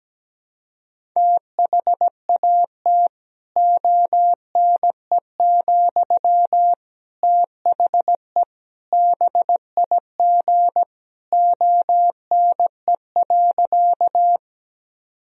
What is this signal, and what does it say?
Morse code, heard as a keyed tone, about 17 wpm, 700 Hz. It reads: THAT ONE, THE BIG ONE.
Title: Sentences from Top 1000 Words - Review Part 1 - 17wpm